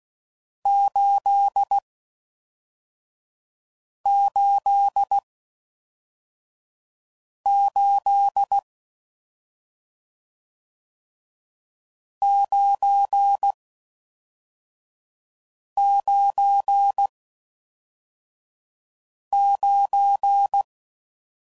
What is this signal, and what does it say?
Morse code, heard as a keyed tone, about 16 wpm, 800 Hz. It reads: 8 8 8 9 9 9